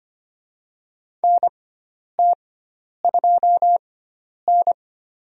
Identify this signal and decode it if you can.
Morse code, heard as a keyed tone, about 25 wpm, spaced as for 10 wpm, 700 Hz. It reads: NT2N
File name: Lesson 16g - Letter c - Call Signs 25wpm